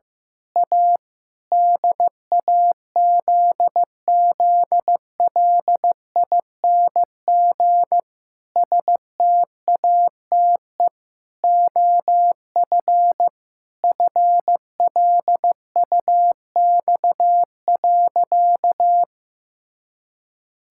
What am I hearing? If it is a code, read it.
Morse code, heard as a keyed tone, about 15 wpm, 700 Hz. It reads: A DAZZLING STATE OF FLUX.
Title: Pangram Challenge v2 - 15wpm